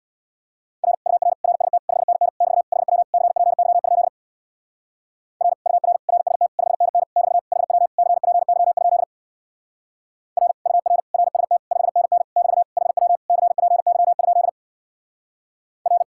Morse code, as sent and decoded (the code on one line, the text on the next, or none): .-. / ..- .-. / .-. ... - / ..... -. -. / -...- / .... .-- / -.-. .--. -.-- ..--.. / .-. / ..- .-. / .-. ... - / ..... -. -. / -...- / .... .-- / -.-. .--. -.-- ..--.. / .-. / ..- .-. / .-. ... - / ..... -. -. / -...- / .... .-- / -.-. .--. -.-- ..--.. / .-.
R UR RST 5NN <BT> HW CPY? R UR RST 5NN <BT> HW CPY? R UR RST 5NN <BT> HW CPY? R